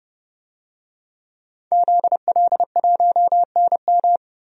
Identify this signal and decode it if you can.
Morse code, heard as a keyed tone, about 30 wpm, 700 Hz. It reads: ZL1NM
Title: DXCC-100 Call Signs - 5 Characters - 30wpm